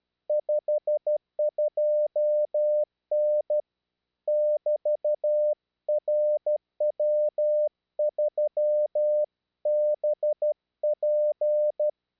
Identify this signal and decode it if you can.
Morse code, heard as a keyed tone, about 12 wpm, 600 Hz. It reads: 52N <BT> RW3BP
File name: eme / rw3bp